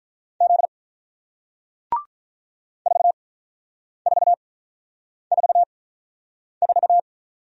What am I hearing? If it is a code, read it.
Morse code, heard as a keyed tone, about 48 wpm, spaced as for 9 wpm, 700 Hz. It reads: Z 4444